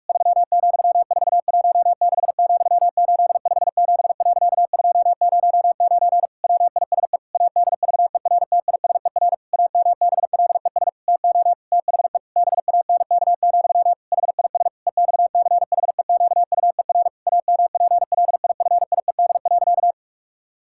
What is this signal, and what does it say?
Morse code, heard as a keyed tone, about 46 wpm, 700 Hz. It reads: 3,416,857.209 WISE ADVERTISER AMBLES TO THE BANK, HIS EXCHEQUER AMPLIFIED.
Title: Pangram Challenge v2 - 45wpm